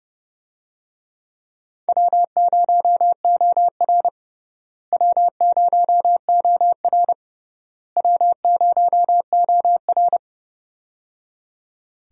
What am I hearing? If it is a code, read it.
Morse code, heard as a keyed tone, about 30 wpm, 700 Hz. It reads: W0OR W0OR W0OR
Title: USA Call Signs - 4 Characters - Repeated 3x - 30wpm